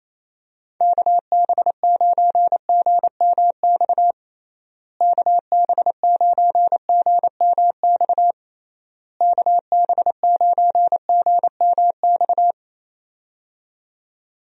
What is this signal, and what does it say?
Morse code, heard as a keyed tone, about 28 wpm, 700 Hz. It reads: KB9GMX KB9GMX KB9GMX
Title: USA Call Signs - 6 Characters - Repeated 3x - 28wpm